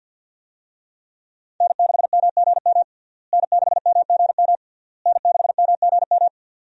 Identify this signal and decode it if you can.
Morse code, heard as a keyed tone, about 52 wpm, 700 Hz. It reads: N6MGM N6MGM N6MGM
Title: USA Call Signs - 5 Characters - Repeated 3x - 50wpm